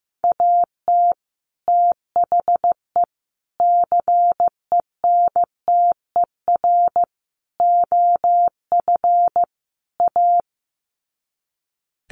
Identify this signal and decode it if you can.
Morse code, heard as a keyed tone, about 15 wpm, 700 Hz. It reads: AT THE CENTER OF A
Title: Sets of 5 Words Encore2 15wpm